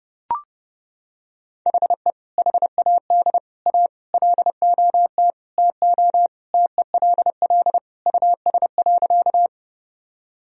Morse code, heard as a keyed tone, about 30 wpm, 700 Hz. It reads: HE HAD A LOT TO TELL US.
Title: Sentences from Top 1000 Words - Review Part 3 - 30wpm